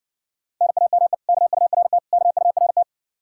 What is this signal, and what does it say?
Morse code, heard as a keyed tone, about 60 wpm, 700 Hz. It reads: NAME KURT KURT